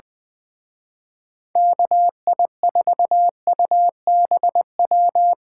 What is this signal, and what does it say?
Morse code, heard as a keyed tone, about 24 wpm, 700 Hz. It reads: KI4UBW
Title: Speed Racing - USA 2x3 Call Signs 20wpm